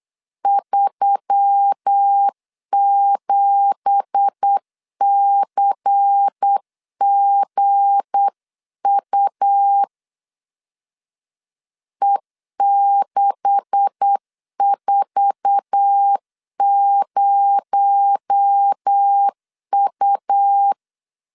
Morse code, heard as a keyed tone, about 8 wpm, 800 Hz. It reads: 37CGU E640U
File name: LEZIONI / 10 SIGNS / 10A SLASH 8wpm 3452